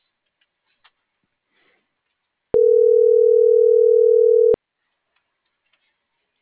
Ringback tone (North America), 440 + 480 Hz.